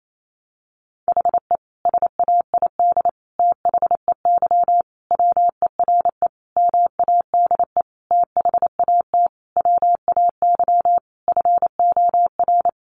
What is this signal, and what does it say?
Morse code, heard as a keyed tone, about 28 wpm, 700 Hz. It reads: HE SAID THEY WERE MADE THAT WAY FOR
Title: Sentences from Top 700 Words 28wpm